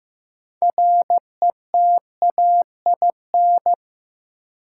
Morse code, heard as a keyed tone, about 15 wpm, 700 Hz. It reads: RETAIN
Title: Six Letter Words - 15wpm